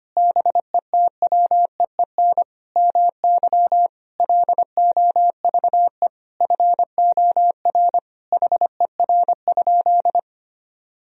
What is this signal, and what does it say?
Morse code, heard as a keyed tone, about 25 wpm, 700 Hz. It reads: BETWEEN MY LOVE FOR HER?
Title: Sentences from Top 1000 Words - Review Part 1 - 25wpm